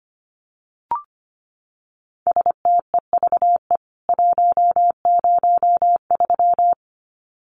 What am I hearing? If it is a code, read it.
Morse code, heard as a keyed tone, about 25 wpm, 700 Hz. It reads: STEVE 103
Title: CWT Exchange 25wpm